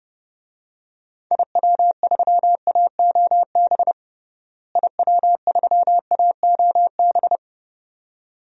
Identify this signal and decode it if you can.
Morse code, heard as a keyed tone, about 30 wpm, 700 Hz. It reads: IW3AOB IW3AOB
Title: DXCC-100 Call Signs - 6 Characters - Repeated 2x - 30wpm